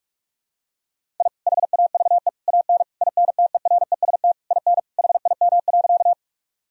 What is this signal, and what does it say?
Morse code, heard as a keyed tone, about 41 wpm, 700 Hz. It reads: I HAVE AN INTEREST IN HIM.
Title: Sentences from Top 1000 Words - Review Part 2 - 45wpm